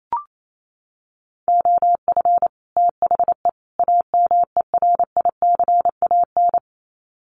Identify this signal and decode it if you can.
Morse code, heard as a keyed tone, about 28 wpm, 700 Hz. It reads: OF THE AMERICAN
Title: Sets of 3 Words 28wpm